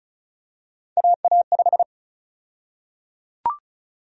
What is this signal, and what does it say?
Morse code, heard as a keyed tone, about 34 wpm, 700 Hz. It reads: AA5
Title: USA Call Sign Prefixes - 35wpm